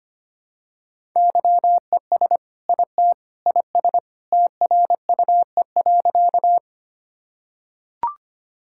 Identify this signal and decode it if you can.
Morse code, heard as a keyed tone, about 25 wpm, 700 Hz. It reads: YES IT IS TRUE.
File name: practice / 25wpm / Sentences with 2-Word Phrases - 25wpm